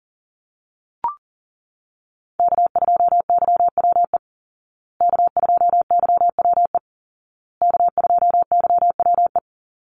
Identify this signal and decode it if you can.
Morse code, heard as a keyed tone, about 40 wpm, 700 Hz. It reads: K2YWE K2YWE K2YWE
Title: Call Signs Repeated 2x - 40wpm